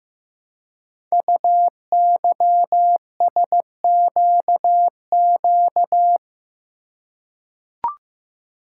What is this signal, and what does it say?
Morse code, heard as a keyed tone, about 15 wpm, 700 Hz. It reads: UYSQQ
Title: Code Groups - 5 Letters - 15wpm